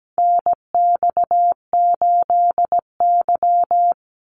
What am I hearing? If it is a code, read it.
Morse code, heard as a keyed tone, about 17 wpm, 700 Hz. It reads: NX8Y
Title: Call Signs - 17wpm